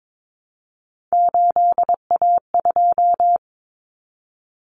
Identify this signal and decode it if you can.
Morse code, heard as a keyed tone, about 22 wpm, 700 Hz. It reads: 8A2
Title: Common Cut Numbers - 3-Digit 22wpm